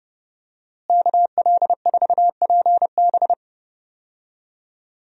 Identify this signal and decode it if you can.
Morse code, heard as a keyed tone, about 30 wpm, 700 Hz. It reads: KL4PB
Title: USA Call Signs - 5 Characters - Repeated 3x - 30wpm